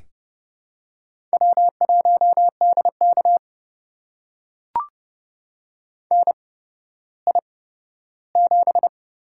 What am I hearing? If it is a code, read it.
Morse code, heard as a keyed tone, about 30 wpm, spaced as for 15 wpm, 700 Hz. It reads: W1DK NI7
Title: USA Call Signs - 8x Character Spacing - 30wpm